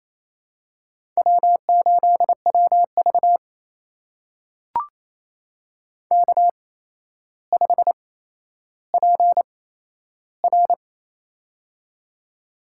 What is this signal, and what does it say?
Morse code, heard as a keyed tone, about 27 wpm, spaced as for 12 wpm, 700 Hz. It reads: W8WV K5PR
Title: USA Call Signs - 8x Character Spacing - 28wpm